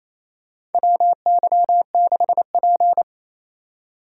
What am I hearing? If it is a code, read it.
Morse code, heard as a keyed tone, about 28 wpm, 700 Hz. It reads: WY6P
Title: USA Call Signs - 4 Characters - 4x Character Spacing - 28wpm